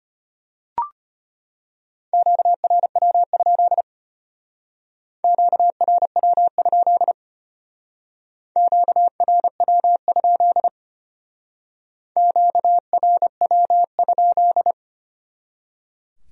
Morse code, heard as a keyed tone, about 32 wpm, 700 Hz. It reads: QRW? QRW? QRW? QRW?